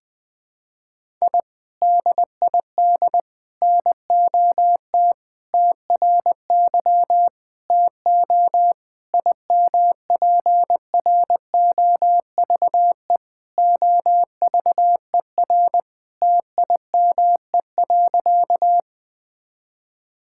Morse code, heard as a keyed tone, about 20 wpm, 700 Hz. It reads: I DID NOT TRY TO IMPROVE OVER TIME.
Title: Sentences from Top 1000 Words - Review Part 5 - 20wpm